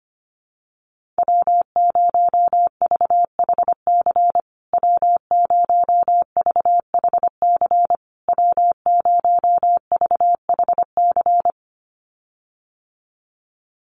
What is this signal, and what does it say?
Morse code, heard as a keyed tone, about 25 wpm, 700 Hz. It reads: W0VHC W0VHC W0VHC
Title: USA Call Signs Repeated 3x 25wpm